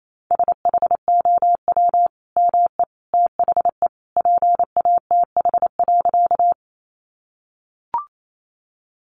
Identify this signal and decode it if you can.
Morse code, heard as a keyed tone, about 28 wpm, 700 Hz. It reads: SHOW ME THE PATH.
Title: Sentences from Top 1400 Words 28wpm